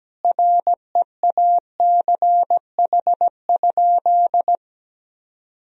Morse code, heard as a keyed tone, about 17 wpm, 700 Hz. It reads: REACH?
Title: Sentences from Top 1000 Words - Review Part 4 - 17wpm